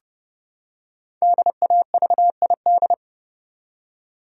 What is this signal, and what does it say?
Morse code, heard as a keyed tone, about 30 wpm, 700 Hz. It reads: DAVID
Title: USA Common Names - Max 5 Letters - 30wpm